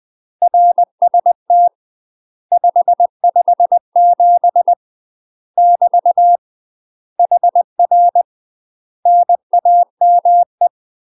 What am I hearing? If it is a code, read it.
Morse code, heard as a keyed tone, about 20 wpm, 700 Hz. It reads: RST 557 <BT> HR NAME
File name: UK MASTER QSOs TRAINING / 20WPM / UK MASTER QSOs 0057